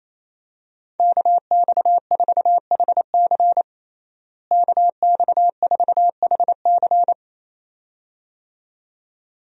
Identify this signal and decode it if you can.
Morse code, heard as a keyed tone, about 27 wpm, 700 Hz. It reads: KX4HC KX4HC
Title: USA Call Signs - 5 Characters - Repeated 2x - 28wpm